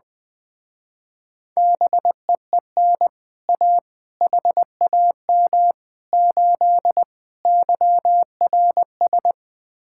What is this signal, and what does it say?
Morse code, heard as a keyed tone, about 20 wpm, 700 Hz. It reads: BEEN A HAM 8 YRS
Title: Speed-Racing Common QSO Phrases - 20wpm